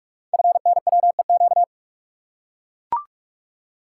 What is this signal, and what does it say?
Morse code, heard as a keyed tone, about 44 wpm, 700 Hz. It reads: FNWEQ
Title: Code Groups - 5 Letters - 45wpm